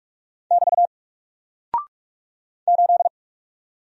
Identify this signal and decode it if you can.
Morse code, heard as a keyed tone, about 44 wpm, 700 Hz. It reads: <BT> 8
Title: Single Character (Letter:Number:Punctuation:Prosign) 45wpm